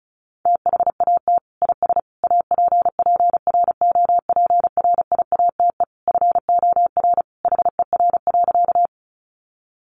Morse code, heard as a keyed tone, about 34 wpm, 700 Hz. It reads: THAT IS APPROPRIATE FOR HER.
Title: Sentences from Top 900 Words 35wpm